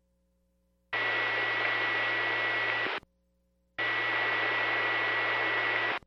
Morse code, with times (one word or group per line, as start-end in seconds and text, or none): none